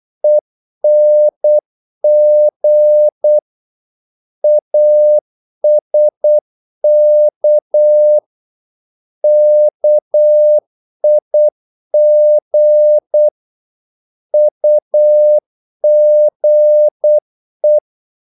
Morse code, heard as a keyed tone, about 8 wpm, 600 Hz. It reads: ENG ASK KIG UGE
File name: Korte ord DK 8wpm